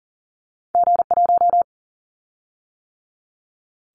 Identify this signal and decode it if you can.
Morse code, heard as a keyed tone, about 40 wpm, 700 Hz. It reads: G1
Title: Callsign Prefixes for Top 25 DXCC - 40wpm